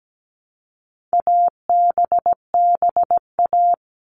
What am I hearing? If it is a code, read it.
Morse code, heard as a keyed tone, about 17 wpm, 700 Hz. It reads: ABBA
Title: All Cut Numbers - 4-Digit 17wpm